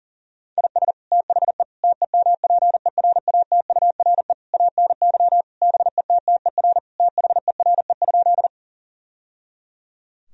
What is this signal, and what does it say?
Morse code, heard as a keyed tone, about 37 wpm, 700 Hz. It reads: IS THE TEMPERATURE ANY BETTER THERE?
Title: Sentences from Top 1000 Words - Review Part 4 - 40wpm